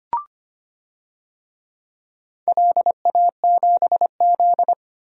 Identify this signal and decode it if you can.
Morse code, heard as a keyed tone, about 25 wpm, 700 Hz. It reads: LA7Z
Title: DXCC-100 Call Signs - 4 Characters - Repeated 3x - 25wpm